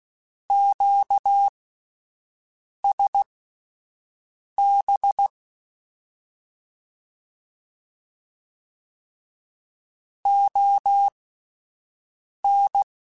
Morse code, heard as a keyed tone, about 16 wpm, spaced as for 5 wpm, 800 Hz. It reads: QSB ON
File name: QSOs / QSOs-mp3s / WD0BGP 1@13